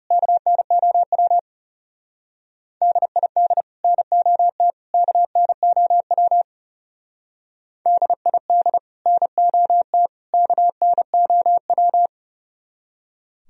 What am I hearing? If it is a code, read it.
Morse code, heard as a keyed tone, about 38 wpm, 700 Hz. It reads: KNOW DID NOT KNOW DID NOT KNOW